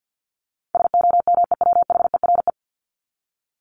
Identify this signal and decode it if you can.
Morse code, heard as a keyed tone, about 50 wpm, 700 Hz. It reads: SOMEWHERE
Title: Top 1251-1500 Words 50wpm